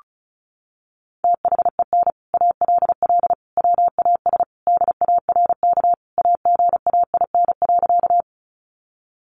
Morse code, heard as a keyed tone, about 34 wpm, 700 Hz. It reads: THEN ALL WAS DARK AGAIN.
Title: Sentences from Top 500 Words 35wpm